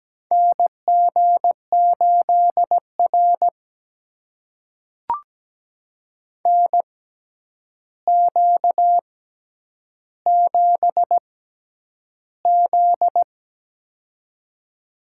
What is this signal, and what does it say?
Morse code, heard as a keyed tone, about 17 wpm, spaced as for 9 wpm, 700 Hz. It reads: NG8R NQ7Z
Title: USA Call Signs - 4 Characters - 6x Character Spacing - 17wpm